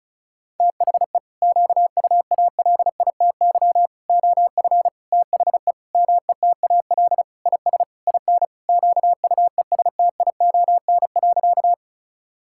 Morse code, heard as a keyed tone, about 34 wpm, 700 Hz. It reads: THE QUALITY OF THE METAL IS IN QUESTION.
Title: Sentences from Top 1000 Words - Review Part 6 - 35wpm